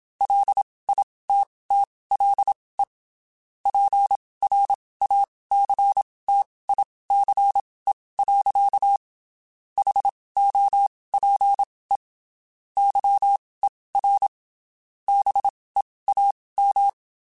Morse code, heard as a keyed tone, about 27 wpm, spaced as for 19 wpm, 800 Hz. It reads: LITTLE PRACTICE. HOPE YER BEAM
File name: (spaced) / QSOs Spaced-mp3s / WD0FBT @25 sp